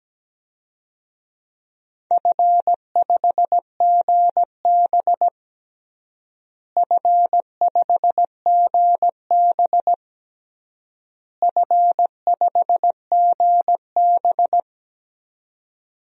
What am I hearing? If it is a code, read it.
Morse code, heard as a keyed tone, about 17 wpm, 700 Hz. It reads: F5GB F5GB F5GB
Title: DXCC-100 Call Signs - 4 Characters - Repeated 3x - 17wpm